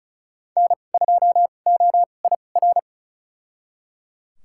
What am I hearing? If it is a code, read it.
Morse code, heard as a keyed tone, about 36 wpm, spaced as for 25 wpm, 700 Hz. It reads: N2OIR
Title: USA Call Signs - 5 Characters - 2x Character Spacing - 35wpm